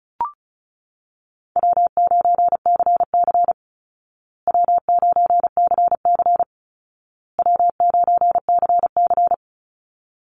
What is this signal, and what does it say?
Morse code, heard as a keyed tone, about 34 wpm, 700 Hz. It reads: W9CC W9CC W9CC
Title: Call Signs Repeated 2x - 35wpm